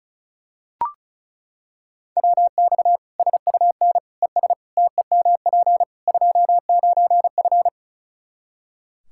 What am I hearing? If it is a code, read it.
Morse code, heard as a keyed tone, about 34 wpm, 700 Hz. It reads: WX SUN ES TEMP 29F
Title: Practice QSO - Exchange 2 - Info - 35wpm